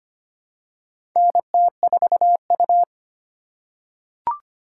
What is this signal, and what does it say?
Morse code, heard as a keyed tone, about 25 wpm, 700 Hz. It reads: NT4U
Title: USA Call Signs - 4 Characters - 2x Character Spacing - 25wpm